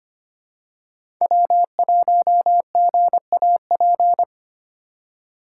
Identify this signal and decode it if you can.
Morse code, heard as a keyed tone, about 25 wpm, 700 Hz. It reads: W1GAP
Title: USA Call Signs - 5 Characters - 4x Character Spacing - 25wpm